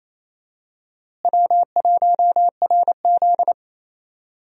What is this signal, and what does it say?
Morse code, heard as a keyed tone, about 28 wpm, 700 Hz. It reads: W1RZ
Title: USA Call Signs - 4 Characters - Repeated 2x - 28wpm